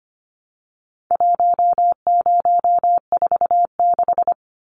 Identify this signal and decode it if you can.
Morse code, heard as a keyed tone, about 25 wpm, 700 Hz. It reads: 1046
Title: Numbers - 4-Digit 25wpm